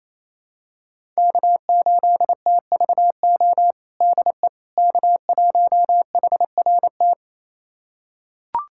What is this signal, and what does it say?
Morse code, heard as a keyed tone, about 27 wpm, 700 Hz. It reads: K8TVO DE K1HRT
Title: USA Callsign DE Callsign - 5 Characters - 28wpm